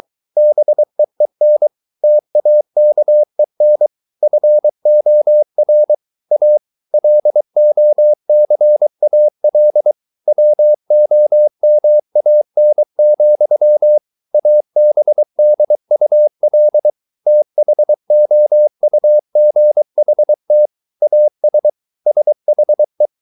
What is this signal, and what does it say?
Morse code, heard as a keyed tone, about 23 wpm, 600 Hz. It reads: BEEN TAKEN FOR A LOCAL WOMAN, ABDUL THOUGHT AS SHE
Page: Romanen Never af Ken Follet på engelsk.